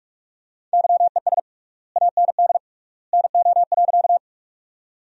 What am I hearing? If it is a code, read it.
Morse code, heard as a keyed tone, about 41 wpm, 700 Hz. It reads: YES AND NO.
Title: 3x Word-Spacing - Sentences from Top 500 Words - 45wpm